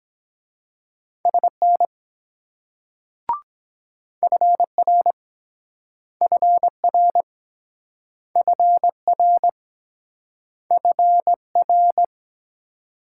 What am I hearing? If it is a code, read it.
Morse code, heard as a keyed tone, about 23 wpm, 700 Hz. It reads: SN FR FR FR FR